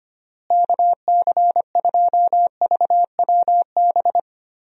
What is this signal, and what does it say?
Morse code, heard as a keyed tone, about 25 wpm, 700 Hz. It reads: KC2VWB
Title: USA Call Signs - 6 Characters - Repeated 3x - 25wpm